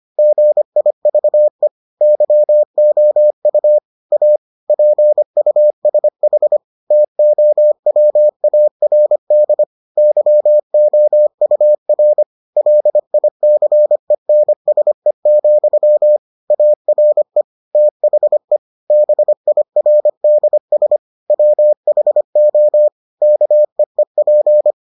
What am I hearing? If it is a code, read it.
Morse code, heard as a keyed tone, about 25 wpm, 600 Hz. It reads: GIVE YOU A PUSH TOWARD YOUR LICENSE, ARE THE BIRDS WHO KEEP